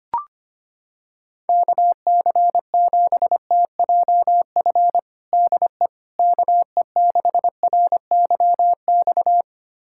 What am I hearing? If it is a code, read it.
Morse code, heard as a keyed tone, about 25 wpm, 700 Hz. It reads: KC7TJF DE KE6RYX
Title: USA Callsign DE Callsign - 6 Characters - 25wpm